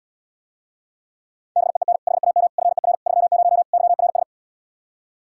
Morse code, heard as a keyed tone, about 80 wpm, 700 Hz. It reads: BEEN HAM FER 30 YRS